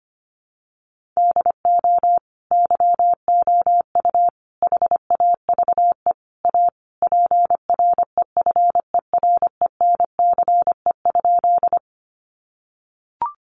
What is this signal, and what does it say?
Morse code, heard as a keyed tone, about 25 wpm, 700 Hz. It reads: DO YOU HAVE A PREFERENCE?
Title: Sentences from Top 1500 Words 25wpm